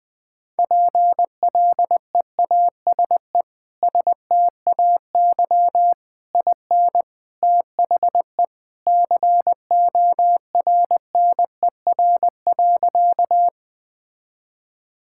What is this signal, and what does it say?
Morse code, heard as a keyed tone, about 20 wpm, 700 Hz. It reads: PLEASE STAY IN THE CORNER.
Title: Sentences from Top 1600 Words 20wpm